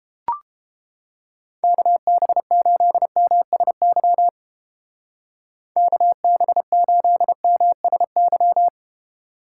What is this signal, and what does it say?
Morse code, heard as a keyed tone, about 29 wpm, 700 Hz. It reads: KB8MSY KB8MSY